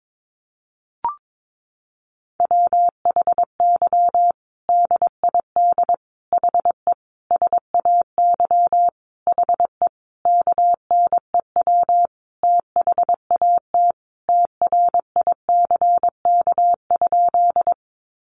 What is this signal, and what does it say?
Morse code, heard as a keyed tone, about 22 wpm, 700 Hz. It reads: WHY DID HE SAY HE KNEW THAT TRICK?
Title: Sentences from Top 1500 Words 22wpm